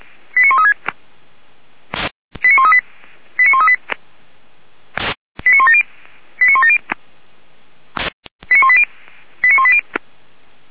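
Five-tone selective call (ZVEI), heard as quick runs of five tones, tones of about 80 ms. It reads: 89136 89138 89138 89170 89170 89180 89180
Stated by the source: ZVEI five-tone call received by SDR, from feuerwehr-orlamuende.org archive (fire brigade Orlamünde).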